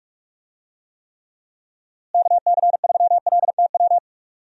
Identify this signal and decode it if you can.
Morse code, heard as a keyed tone, about 46 wpm, 700 Hz. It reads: KC3LTW
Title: USA Call Signs - 6 Characters - Repeated 3x - 45wpm